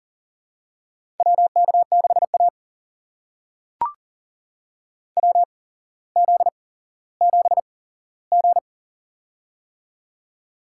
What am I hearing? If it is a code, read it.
Morse code, heard as a keyed tone, about 40 wpm, spaced as for 17 wpm, 700 Hz. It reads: WK6A WZ7G
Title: USA Call Signs - 4 Characters - 8x Character Spacing - 40wpm